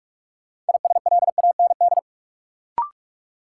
Morse code, heard as a keyed tone, about 46 wpm, 700 Hz. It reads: ISLAND
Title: Sentences from Top 500 Words - New Word Drill - 45wpm